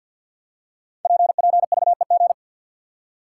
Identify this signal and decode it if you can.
Morse code, heard as a keyed tone, about 46 wpm, 700 Hz. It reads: PPVEG
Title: Code Groups - 5 Letter:Numbers - 50wpm